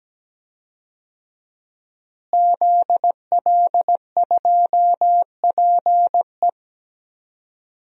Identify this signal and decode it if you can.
Morse code, heard as a keyed tone, about 17 wpm, 700 Hz. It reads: ZL2PE
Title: DXCC-100 Call Signs - 5 Characters - Repeated 3x - 17wpm